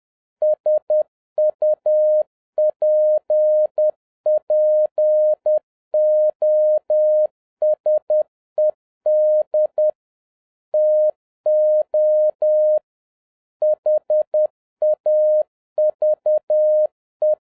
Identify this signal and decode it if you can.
Morse code, heard as a keyed tone, about 10 wpm, 610 Hz. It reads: SUPPOSED TO HAVE
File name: 2011-07-mp3 / n31 THE 3.11 2 10wpm